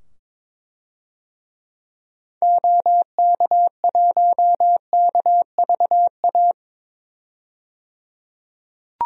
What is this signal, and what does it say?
Morse code, heard as a keyed tone, about 21 wpm, 700 Hz. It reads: OK1KVA
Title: DXCC-100 Call Signs - 6 Characters - 22wpm